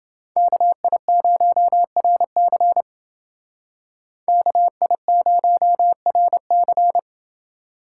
Morse code, heard as a keyed tone, about 31 wpm, 700 Hz. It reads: KI0RC KI0RC